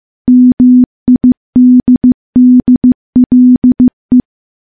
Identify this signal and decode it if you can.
Morse code, heard as a keyed tone, about 14 wpm, 250 Hz. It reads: MIDDLE
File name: morse / morse-training-15WPM